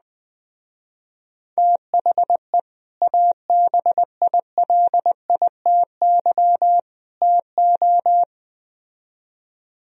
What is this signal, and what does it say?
Morse code, heard as a keyed tone, about 20 wpm, 700 Hz. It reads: THE ABILITY TO
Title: Speed-Racing - Sets of 3 Words 17wpm